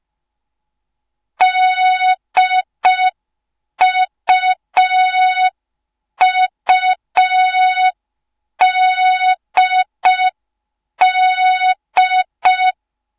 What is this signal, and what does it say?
Morse code, heard as a keyed tone, about 5 wpm, 3700 Hz. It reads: DUUDD